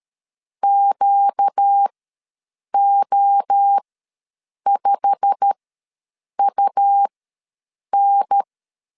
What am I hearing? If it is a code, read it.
Morse code, heard as a keyed tone, about 13 wpm, spaced as for 7 wpm, 800 Hz. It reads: QO5UN
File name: LEZIONI / 8 YJ45 / 8 YJ45 ws 12wpm 3146